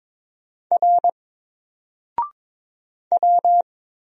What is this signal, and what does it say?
Morse code, heard as a keyed tone, about 22 wpm, 700 Hz. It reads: R W
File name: Single-Letter-Number-22wpm